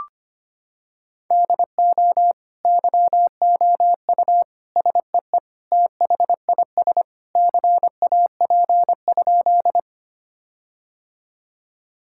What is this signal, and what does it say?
Morse code, heard as a keyed tone, about 25 wpm, 700 Hz. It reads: DO YOU SEE THIS CAP?
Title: Sentences from Top 1600 Words 25wpm